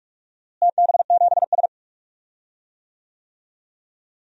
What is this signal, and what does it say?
Morse code, heard as a keyed tone, about 46 wpm, 700 Hz. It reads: TB7S
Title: DXCC-100 Call Signs - 4 Characters - Repeated 3x - 45wpm